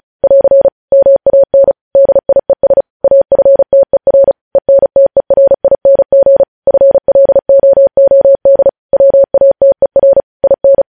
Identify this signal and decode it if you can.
Morse code, heard as a keyed tone, about 36 wpm, 550 Hz. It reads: <AR> MAN DIES AFTER ENTERING FLOOD WATER IN